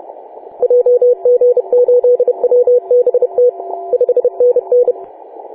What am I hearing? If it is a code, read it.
Morse code, heard as a keyed tone, about 30 wpm, 490 Hz. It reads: JG8WBT 5NN
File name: 4W/G3ZEM 28.010MHz CW